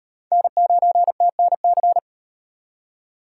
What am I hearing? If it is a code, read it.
Morse code, heard as a keyed tone, about 38 wpm, 700 Hz. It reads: N9TNC